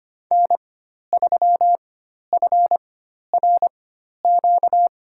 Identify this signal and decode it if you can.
Morse code, heard as a keyed tone, about 25 wpm, spaced as for 12 wpm, 700 Hz. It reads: N3FRQ
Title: USA Call Signs - 4x Character Spacing - 25wpm